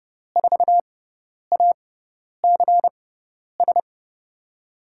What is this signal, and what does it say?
Morse code, heard as a keyed tone, about 31 wpm, spaced as for 10 wpm, 700 Hz. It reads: 4ACS
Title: USA Call Signs - 6x Character Spacing - 30wpm